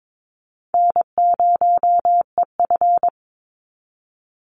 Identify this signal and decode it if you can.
Morse code, heard as a keyed tone, about 22 wpm, 700 Hz. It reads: N0EF
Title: Easy Call Signs Repeated 3x - 22wpm